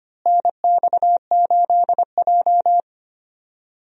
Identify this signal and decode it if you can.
Morse code, heard as a keyed tone, about 25 wpm, 700 Hz. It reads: NX8J
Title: USA Call Signs - 4 Characters - Repeated 3x - 25wpm